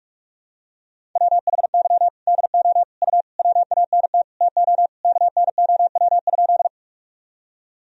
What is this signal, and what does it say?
Morse code, heard as a keyed tone, about 41 wpm, 700 Hz. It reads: WHY DO U WANT TO KNOW?
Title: Sentences from Top 200 Words and Abbreviations 45wpm